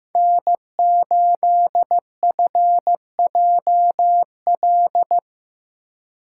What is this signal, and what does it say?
Morse code, heard as a keyed tone, about 18 wpm, 700 Hz. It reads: N8FJL
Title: Speed Racing - USA 1x3 Call Signs 15wpm